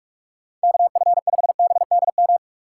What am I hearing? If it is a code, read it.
Morse code, heard as a keyed tone, about 46 wpm, 700 Hz. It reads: KF5BDM
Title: USA Call Signs - 6 Characters - 6x Character Spacing - 45wpm